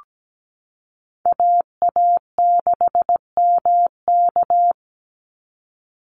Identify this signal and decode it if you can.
Morse code, heard as a keyed tone, about 17 wpm, 700 Hz. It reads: AA6MK
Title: Call Signs Repeated 2x - 17wpm